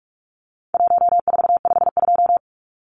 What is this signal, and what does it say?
Morse code, heard as a keyed tone, about 44 wpm, 700 Hz. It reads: 1452
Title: Numbers - Contest Style - 4-Digit 45wpm